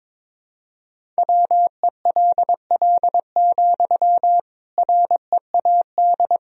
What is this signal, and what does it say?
Morse code, heard as a keyed tone, about 22 wpm, 700 Hz. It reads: WELL, READ
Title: Sentences from Top 300 Words and Abbreviations 22wpm